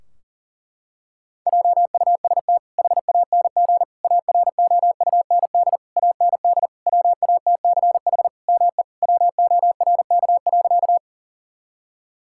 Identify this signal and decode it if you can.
Morse code, heard as a keyed tone, about 40 wpm, 700 Hz. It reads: JUST HANG AROUND AND WATCH ME WORK.
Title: Sentences from Top 1600 Words 40wpm